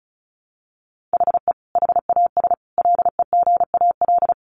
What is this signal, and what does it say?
Morse code, heard as a keyed tone, about 34 wpm, 700 Hz. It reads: HE HAS LEGAL
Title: Sentences from Top 1200 Words 35wpm